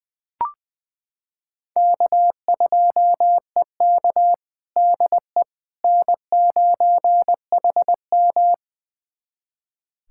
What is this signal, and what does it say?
Morse code, heard as a keyed tone, about 20 wpm, 700 Hz. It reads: K2EK DE N9HM
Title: USA Callsign DE Callsign - 4 Characters - 20wpm